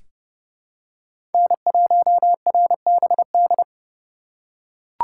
Morse code, heard as a keyed tone, about 31 wpm, 700 Hz. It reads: N1RBD
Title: K1USN SST QSO Practice - 30wpm